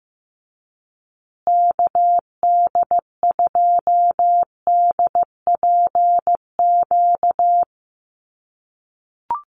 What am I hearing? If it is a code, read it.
Morse code, heard as a keyed tone, about 15 wpm, 700 Hz. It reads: KD2DPQ
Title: USA Call Signs Repeated 2x 15wpm